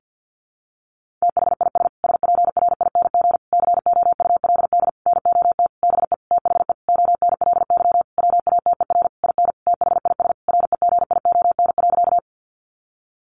Morse code, heard as a keyed tone, about 50 wpm, 700 Hz. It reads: THIS SPRING COULD NOT BE THE ONLY WATER IN THIS REGION.
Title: Sentences from Top 1500 Words 50wpm